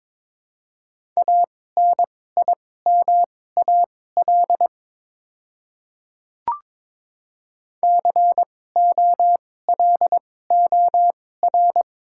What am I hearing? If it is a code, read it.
Morse code, heard as a keyed tone, about 22 wpm, spaced as for 16 wpm, 700 Hz. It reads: ANIMAL COLOR
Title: 2x Character Spacing - Top 300 Words - 22wpm